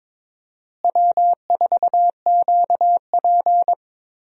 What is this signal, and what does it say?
Morse code, heard as a keyed tone, about 21 wpm, 700 Hz. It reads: W4QP
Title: USA Call Signs - 4 Characters - Repeated 3x - 22wpm